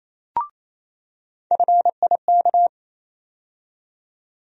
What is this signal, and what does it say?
Morse code, heard as a keyed tone, about 27 wpm, 700 Hz. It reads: FIK
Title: USA Call Sign Postfixes - 3 Letters 28wpm